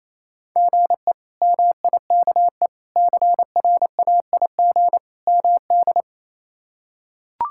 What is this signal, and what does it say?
Morse code, heard as a keyed tone, about 28 wpm, 700 Hz. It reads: GE MIKE CRAIG MD
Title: K1USN SST QSO Practice - 28wpm